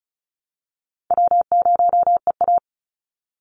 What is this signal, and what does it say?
Morse code, heard as a keyed tone, about 34 wpm, 700 Hz. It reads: W0EA
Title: Call Signs Repeated 3x - 35wpm